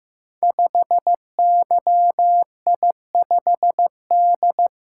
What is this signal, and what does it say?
Morse code, heard as a keyed tone, about 15 wpm, 700 Hz. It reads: 5YI5D
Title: Code Groups - 5 Letter:Numbers - 15wpm